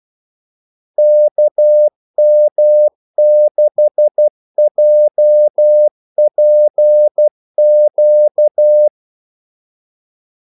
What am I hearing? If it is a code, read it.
Morse code, heard as a keyed tone, about 12 wpm, 600 Hz. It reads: KM6JPQ
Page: 2 ekstra ordmellemrum mellem kaldesignalerne